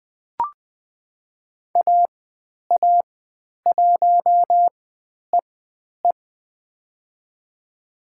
Morse code, heard as a keyed tone, about 20 wpm, spaced as for 10 wpm, 700 Hz. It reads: AA1EE
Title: 109 Lesson 18g - Letter m - Call Signs 20wpm